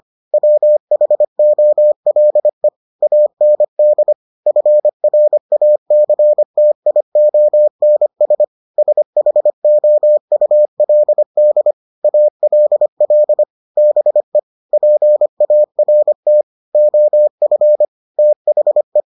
Morse code, heard as a keyed tone, about 25 wpm, 600 Hz. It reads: WHOLE AND FRACTIONS SHOULD ALL BE PART OF THE